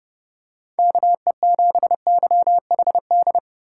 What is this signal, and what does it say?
Morse code, heard as a keyed tone, about 31 wpm, 700 Hz. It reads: KE7YHD